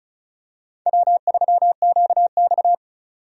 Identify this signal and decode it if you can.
Morse code, heard as a keyed tone, about 36 wpm, 700 Hz. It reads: W3QX
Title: USA Call Signs - 4 Characters - 35wpm